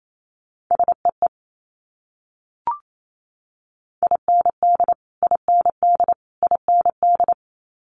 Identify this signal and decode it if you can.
Morse code, heard as a keyed tone, about 28 wpm, 700 Hz. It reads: SEE IND IND IND
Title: Multi-letter Phonemes Repeated 3x 28wpm